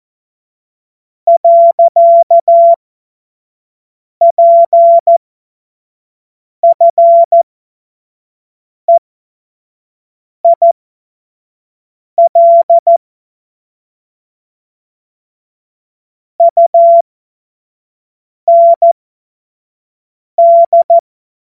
Morse code, heard as a keyed tone, about 14 wpm, spaced as for 5 wpm, 680 Hz. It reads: . PFEIL UND
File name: Morsekurs / Witze / Witz 15. August 2023 25 F70